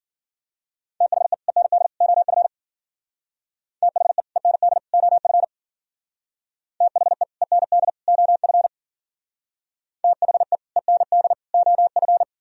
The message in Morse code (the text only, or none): THE END OF THE END OF THE END OF THE END OF